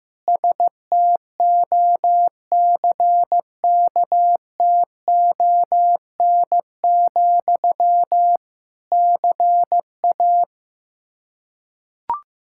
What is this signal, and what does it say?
Morse code, heard as a keyed tone, about 15 wpm, 700 Hz. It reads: STOCKTON, CA
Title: K1USN SST QSO Practice 15wpm